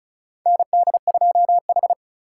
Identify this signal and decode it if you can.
Morse code, heard as a keyed tone, about 36 wpm, 700 Hz. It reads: ND2H
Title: USA Call Signs - 4 Characters - 2x Character Spacing - 35wpm